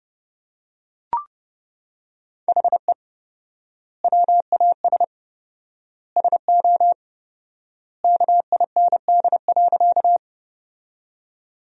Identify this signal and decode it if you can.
Morse code, heard as a keyed tone, about 30 wpm, 700 Hz. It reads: HE WAS SO KIND.
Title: 4x Word-Spacing - Sentences from Top 200 Words - 30wpm